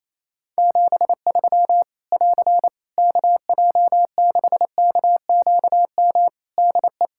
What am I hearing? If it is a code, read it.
Morse code, heard as a keyed tone, about 28 wpm, 700 Hz. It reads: 73 <AR> KJ6KQM DE
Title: Practice QSO - Exchange 4 - Ending - 28wpm